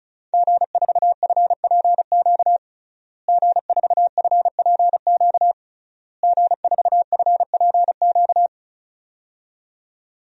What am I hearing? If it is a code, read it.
Morse code, heard as a keyed tone, about 36 wpm, 700 Hz. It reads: G4FPQ G4FPQ G4FPQ
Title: DXCC-100 Call Signs - 5 Characters - Repeated 3x - 35wpm